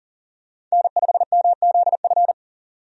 Morse code, heard as a keyed tone, about 40 wpm, 700 Hz. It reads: N5MZF
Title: USA Call Signs - 5 Characters - 40wpm